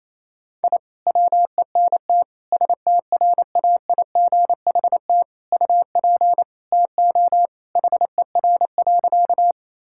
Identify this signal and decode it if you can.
Morse code, heard as a keyed tone, about 28 wpm, 700 Hz. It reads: I WENT STRAIGHT UP TO HER.
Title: Sentences from Top 1000 Words - Review Part 6 - 28wpm